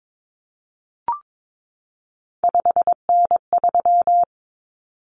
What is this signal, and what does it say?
Morse code, heard as a keyed tone, about 22 wpm, 700 Hz. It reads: 5N3
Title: Common Cut Numbers - 3-Digit 22wpm